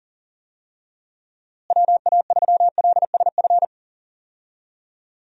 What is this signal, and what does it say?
Morse code, heard as a keyed tone, about 40 wpm, 700 Hz. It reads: WA3LSF
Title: USA Call Signs - 6 Characters - Repeated 2x - 40wpm